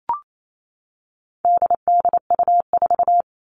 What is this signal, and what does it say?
Morse code, heard as a keyed tone, about 28 wpm, 700 Hz. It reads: DDU4
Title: All Cut Numbers - 4-Digit 28wpm